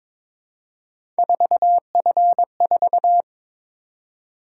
Morse code, heard as a keyed tone, about 22 wpm, 700 Hz. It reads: 4F4
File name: practice / 22wpm / Three Letter-Number Combinations - 22wpm